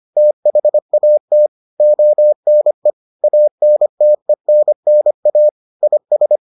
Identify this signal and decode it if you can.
Morse code, heard as a keyed tone, about 25 wpm, 600 Hz. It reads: THAT ONE ANTENNA IS